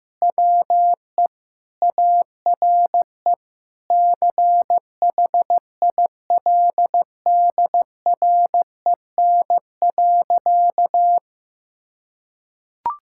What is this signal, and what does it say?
Morse code, heard as a keyed tone, about 16 wpm, 700 Hz. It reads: WE ARE CHILDREN.
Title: Sentences from Top 1000 Words - Review Part 6 - 15wpm